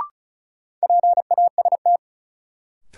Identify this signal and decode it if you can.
Morse code, heard as a keyed tone, about 36 wpm, 700 Hz. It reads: PAST
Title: Sentences from Top 600 Words - New Word Drill - 35wpm